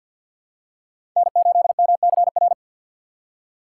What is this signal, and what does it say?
Morse code, heard as a keyed tone, about 53 wpm, 700 Hz. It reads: N8MCR
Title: USA Call Signs - 5 Characters - 6x Character Spacing - 50wpm